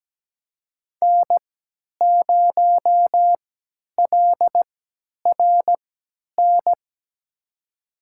Morse code, heard as a keyed tone, about 17 wpm, spaced as for 10 wpm, 700 Hz. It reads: N0LRN
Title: USA Call Signs - 5 Characters - 3x Character Spacing - 17wpm